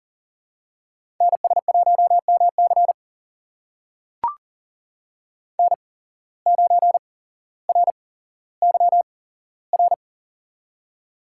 Morse code, heard as a keyed tone, about 38 wpm, spaced as for 17 wpm, 700 Hz. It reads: NS1MC N9RYR
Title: USA Call Signs - 8x Character Spacing - 40wpm